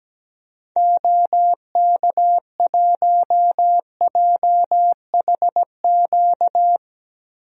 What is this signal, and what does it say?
Morse code, heard as a keyed tone, about 17 wpm, 700 Hz. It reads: OK1JHQ
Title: DXCC-100 Call Signs - 6 Characters - Repeated 2x - 17wpm